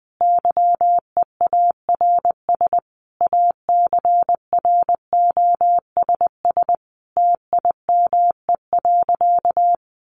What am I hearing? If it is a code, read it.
Morse code, heard as a keyed tone, about 20 wpm, 700 Hz. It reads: YEARS ACROSS TIME.
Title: A Sound Of Thunder 20wpm